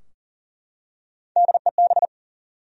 Morse code, heard as a keyed tone, about 37 wpm, 700 Hz. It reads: DEB
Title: USA Common Names - 40wpm